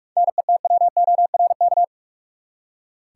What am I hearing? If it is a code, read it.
Morse code, heard as a keyed tone, about 42 wpm, 700 Hz. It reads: NETWORK
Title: Seven Letter Words - 45wpm